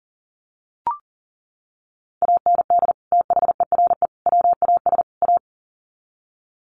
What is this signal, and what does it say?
Morse code, heard as a keyed tone, about 40 wpm, 700 Hz. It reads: AND THERE WAS A
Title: Sets of 4 Words 40wpm